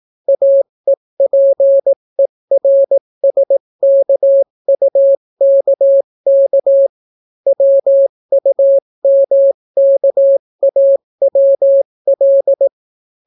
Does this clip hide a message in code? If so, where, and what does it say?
Yes: Morse code from 0.3 to 12.7 s, heard as a keyed tone, about 18 wpm, 550 Hz. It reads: AEPERSKUKK WUMKAWL